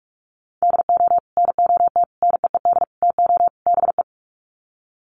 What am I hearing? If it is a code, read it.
Morse code, heard as a keyed tone, about 44 wpm, 700 Hz. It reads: DO NOT NEED TO BE
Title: Sets of 5 Words Encore 45wpm